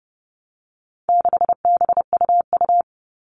Morse code, heard as a keyed tone, about 30 wpm, 700 Hz. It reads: 6BUU
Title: All Cut Numbers - 4-Digit 30wpm